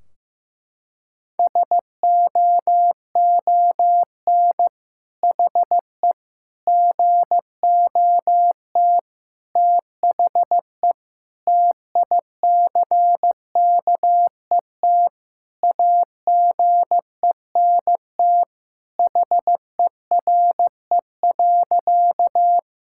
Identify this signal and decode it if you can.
Morse code, heard as a keyed tone, about 15 wpm, 700 Hz. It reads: SOON HE GOT THE TICKET AGENT HERE.
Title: Sentences from Top 1600 Words 15wpm